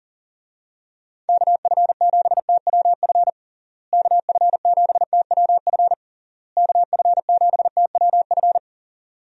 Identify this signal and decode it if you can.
Morse code, heard as a keyed tone, about 40 wpm, 700 Hz. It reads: KF7TWF KF7TWF KF7TWF
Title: USA Call Signs - 6 Characters - Repeated 3x - 40wpm